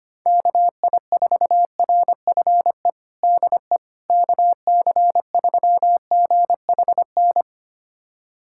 Morse code, heard as a keyed tone, about 25 wpm, 700 Hz. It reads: KI4RFE DE KC3GHN
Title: USA Callsign DE Callsign - 6 Characters - 25wpm